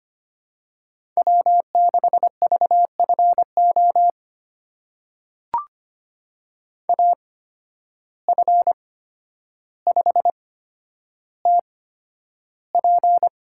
Morse code, heard as a keyed tone, about 25 wpm, spaced as for 11 wpm, 700 Hz. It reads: W6VFO AF5TP
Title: USA Call Signs - 5 Characters - 8x Character Spacing - 25wpm